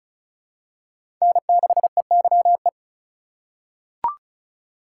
Morse code, heard as a keyed tone, about 36 wpm, 700 Hz. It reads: N6EYE
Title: USA Call Signs - 5 Characters - 4x Character Spacing - 35wpm